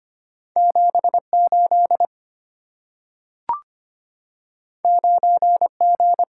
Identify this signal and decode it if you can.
Morse code, heard as a keyed tone, about 25 wpm, 690 Hz. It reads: 78 9G
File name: Two Letter-Number Combinations - 25wpm